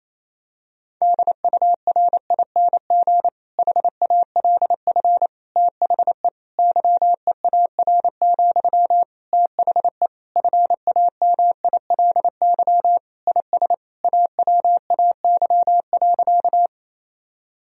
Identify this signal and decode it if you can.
Morse code, heard as a keyed tone, about 28 wpm, 700 Hz. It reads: DURING HALF THE YEAR, THE FAMILY IS AWAY.
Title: Sentences from Top 1000 Words - Review Part 2 - 28wpm